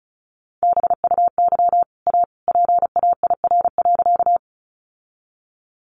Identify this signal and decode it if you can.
Morse code, heard as a keyed tone, about 34 wpm, 700 Hz. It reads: BUY A PAIR.
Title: Sentences from Top 1200 Words 35wpm